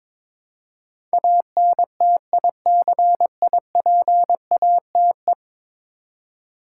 Morse code, heard as a keyed tone, about 22 wpm, 700 Hz. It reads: ANTICIPATE
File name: practice / 22wpm / 10 Letter Words - First 3 Letters Given - 22wpm